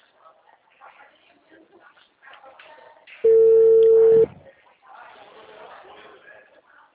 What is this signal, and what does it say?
Ringback tone (China, 450 Hz).
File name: out- -2003